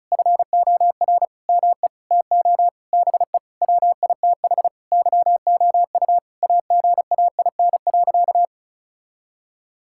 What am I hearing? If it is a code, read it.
Morse code, heard as a keyed tone, about 36 wpm, 700 Hz. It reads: FOR ME TO BE WITH YOU AGAIN.
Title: Sentences from Top 1600 Words 35wpm